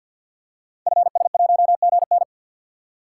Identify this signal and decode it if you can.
Morse code, heard as a keyed tone, about 51 wpm, 700 Hz. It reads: FS1GN
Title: Code Groups - 5 Letter:Numbers - 50wpm